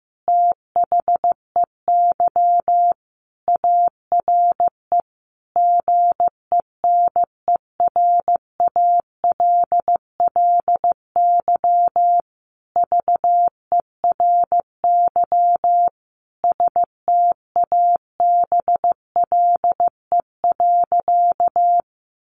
Morse code, heard as a keyed tone, about 15 wpm, 700 Hz. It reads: THEY ARE GENERALLY VERY STABLE.
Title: Sentences from Top 1500 Words 15wpm